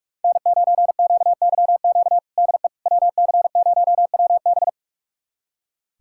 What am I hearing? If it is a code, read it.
Morse code, heard as a keyed tone, about 43 wpm, 700 Hz. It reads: N9QYQ DE WC0WB
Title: USA Callsign DE Callsign - 5 Characters - 45wpm